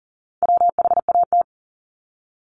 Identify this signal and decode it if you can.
Morse code, heard as a keyed tone, about 38 wpm, 700 Hz. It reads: WHAT
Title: Multi-letter Phonemes Repeated 3x 40wpm